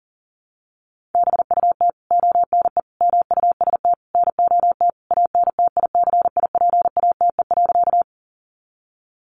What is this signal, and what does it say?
Morse code, heard as a keyed tone, about 40 wpm, 700 Hz. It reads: BUT ONE MUST NOT ANTICIPATE.
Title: Sentences from Top 1500 Words 40wpm